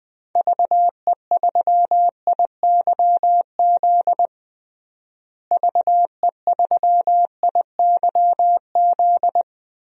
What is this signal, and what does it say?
Morse code, heard as a keyed tone, about 20 wpm, 700 Hz. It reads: VE3IYZ VE3IYZ
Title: DXCC-100 Call Signs - 6 Characters - Repeated 3x - 20wpm